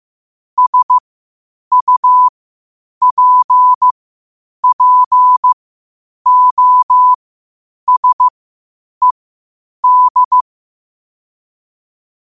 Morse code, heard as a keyed tone, about 15 wpm, spaced as for 8 wpm, 1000 Hz. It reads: SUPPOSED
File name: qotdfiles / 8wpm